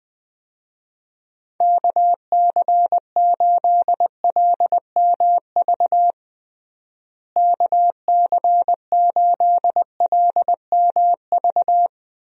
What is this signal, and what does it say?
Morse code, heard as a keyed tone, about 20 wpm, 700 Hz. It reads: KC8LMV KC8LMV
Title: USA Call Signs - 6 Characters - Repeated 2x - 20wpm